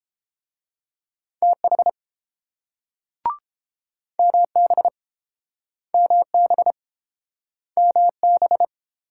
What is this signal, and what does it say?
Morse code, heard as a keyed tone, about 30 wpm, 690 Hz. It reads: TH MB MB MB